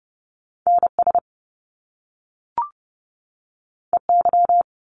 Morse code, heard as a keyed tone, about 30 wpm, 700 Hz. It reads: NS EY
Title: Top Two Letter Combinations in Words 30wpm